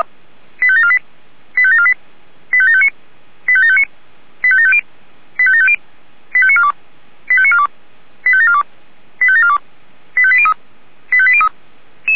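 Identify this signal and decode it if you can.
Five-tone selective call (ZVEI), heard as quick runs of five tones, tones of about 70 ms. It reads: 85748 85748 85759 85759 85750 85750 85942 85942 85742 85742 85803 85803
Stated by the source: ZVEI five-tone selective call received by SDR.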